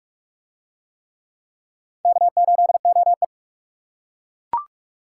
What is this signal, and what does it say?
Morse code, heard as a keyed tone, about 41 wpm, 700 Hz. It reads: K8OE